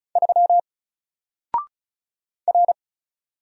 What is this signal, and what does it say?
Morse code, heard as a keyed tone, about 36 wpm, 700 Hz. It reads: 3 R
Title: Single Character (Letter:Number:Punctuation:Prosign) 35wpm